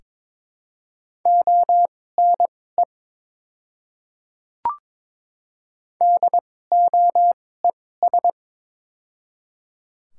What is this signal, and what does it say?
Morse code, heard as a keyed tone, about 22 wpm, spaced as for 16 wpm, 700 Hz. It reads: ONE DOES
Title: 2x Character Spacing - Top 500 Words - 22wpm